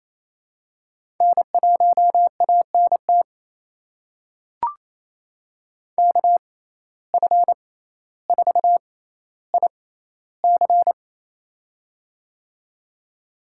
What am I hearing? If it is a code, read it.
Morse code, heard as a keyed tone, about 28 wpm, spaced as for 14 wpm, 700 Hz. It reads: N1ANT KF4IC
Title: USA Call Signs - 5 Characters - 6x Character Spacing - 28wpm